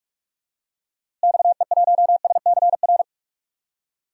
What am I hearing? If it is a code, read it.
Morse code, heard as a keyed tone, about 46 wpm, 700 Hz. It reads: XE1SCR
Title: DXCC-100 Call Signs - 6 Characters - Repeated 3x - 45wpm